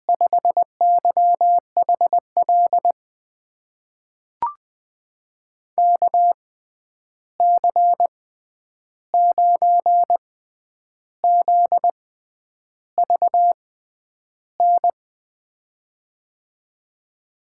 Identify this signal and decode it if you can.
Morse code, heard as a keyed tone, about 20 wpm, spaced as for 9 wpm, 700 Hz. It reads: 5YHL KC9ZVN
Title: USA Call Signs - 6 Characters - 6x Character Spacing - 20wpm